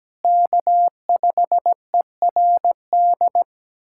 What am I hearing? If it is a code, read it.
Morse code, heard as a keyed tone, about 17 wpm, 700 Hz. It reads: K5ERD
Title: USA Callsign DE Callsign - 5 Characters - 17wpm